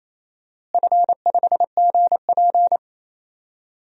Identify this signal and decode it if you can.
Morse code, heard as a keyed tone, about 28 wpm, 700 Hz. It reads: F5GP
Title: DXCC-100 Call Signs - 4 Characters - Repeated 2x - 28wpm